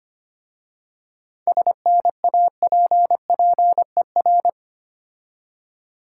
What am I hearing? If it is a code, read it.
Morse code, heard as a keyed tone, about 25 wpm, 700 Hz. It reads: SNAPPER
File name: practice / 25wpm / Difficult Words with Hints - 25wpm